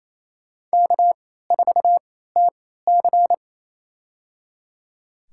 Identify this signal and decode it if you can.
Morse code, heard as a keyed tone, about 27 wpm, spaced as for 16 wpm, 700 Hz. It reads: K4TC
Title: USA Call Signs - 4 Characters - 3x Character Spacing - 28wpm